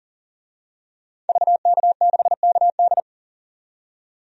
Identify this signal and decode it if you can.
Morse code, heard as a keyed tone, about 38 wpm, 700 Hz. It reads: VK6KD
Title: DXCC-100 Call Signs - 5 Characters - 40wpm